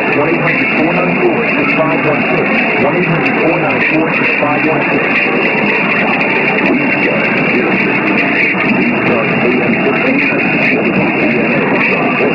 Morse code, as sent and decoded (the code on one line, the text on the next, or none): none